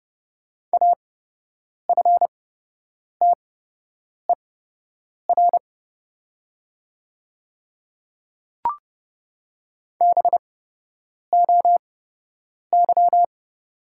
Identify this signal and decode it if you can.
Morse code, heard as a keyed tone, about 29 wpm, spaced as for 8 wpm, 690 Hz. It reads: AFTER BOY
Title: 8x Character Spacing - Top 200 Words - 30wpm